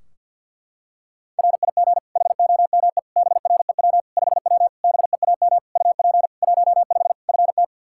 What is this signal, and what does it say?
Morse code, heard as a keyed tone, about 52 wpm, 700 Hz. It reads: RIG HOME BREW 5W BEAM UP 15 FT
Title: K1USN SST QSO Practice 50wpm